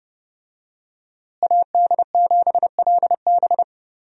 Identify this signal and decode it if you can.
Morse code, heard as a keyed tone, about 30 wpm, 700 Hz. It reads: AD7LB
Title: USA Call Signs - 5 Characters - Repeated 3x - 30wpm